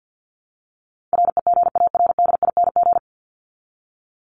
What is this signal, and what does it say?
Morse code, heard as a keyed tone, about 49 wpm, 700 Hz. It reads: REGARDING
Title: ING Suffix - Top 100 Words with Base Words 50wpm